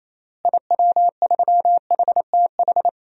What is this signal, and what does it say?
Morse code, heard as a keyed tone, about 28 wpm, 700 Hz. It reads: IW3HTH
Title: DXCC-100 Call Signs - 6 Characters - Repeated 3x - 28wpm